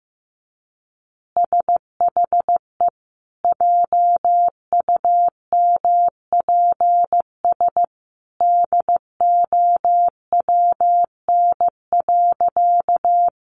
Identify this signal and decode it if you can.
Morse code, heard as a keyed tone, about 15 wpm, 700 Hz. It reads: SHE JUMPS DOWN.
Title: Easy Phrases 15wpm